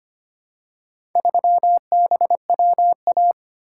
Morse code, heard as a keyed tone, about 25 wpm, 700 Hz. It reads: 3BWA
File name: practice / 25wpm / Four Letter-Number Combinations - 25wpm